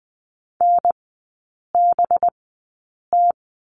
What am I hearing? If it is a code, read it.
Morse code, heard as a keyed tone, about 20 wpm, spaced as for 8 wpm, 700 Hz. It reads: NBT
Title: 218 Lesson 35e - Number 0 - Three Characters 20wpm